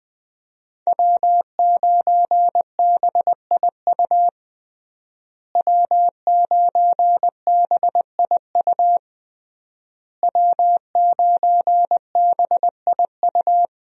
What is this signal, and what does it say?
Morse code, heard as a keyed tone, about 20 wpm, 700 Hz. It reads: W9BIU W9BIU W9BIU
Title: USA Call Signs - 5 Characters - Repeated 3x - 20wpm